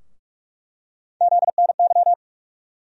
Morse code, heard as a keyed tone, about 43 wpm, 700 Hz. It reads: ZNY
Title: USA Call Sign Postfixes - 3 Letters 45wpm